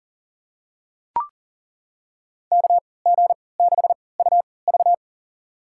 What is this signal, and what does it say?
Morse code, heard as a keyed tone, about 40 wpm, spaced as for 23 wpm, 700 Hz. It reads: KG6UV
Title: USA Call Signs - 5 Characters - 3x Character Spacing - 40wpm